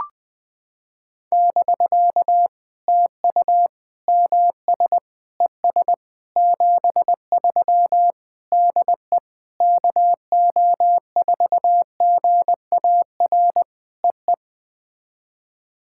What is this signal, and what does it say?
Morse code, heard as a keyed tone, about 20 wpm, 700 Hz. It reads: <BK> TU MS ES 73 DE KO4GAR EE
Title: POTA QSO Practice - 20wpm